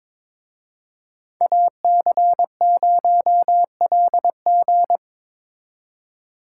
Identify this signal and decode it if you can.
Morse code, heard as a keyed tone, about 25 wpm, 700 Hz. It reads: AC0LG
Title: Speed Racing - USA 2x2 Call Signs 22wpm